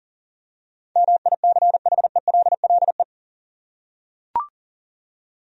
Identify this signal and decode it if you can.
Morse code, heard as a keyed tone, about 40 wpm, 700 Hz. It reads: MICHELLE
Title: USA Common Names - 40wpm